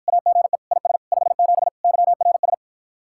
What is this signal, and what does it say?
Morse code, heard as a keyed tone, about 52 wpm, 700 Hz. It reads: AGE IS 57 YRS